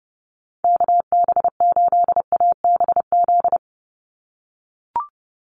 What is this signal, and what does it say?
Morse code, heard as a keyed tone, about 30 wpm, 700 Hz. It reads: KB8ABZ
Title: USA Call Signs 30wpm